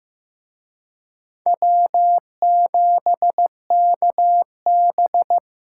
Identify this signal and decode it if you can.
Morse code, heard as a keyed tone, about 18 wpm, 700 Hz. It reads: W7KB
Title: Speed Racing - USA 1x2 Call Signs 15wpm